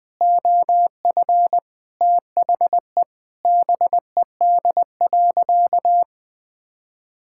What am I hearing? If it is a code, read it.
Morse code, heard as a keyed tone, about 20 wpm, 700 Hz. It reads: OF THE BED.
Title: Sentences with 5-Word Phrases - Spoken Beginning - 20wpm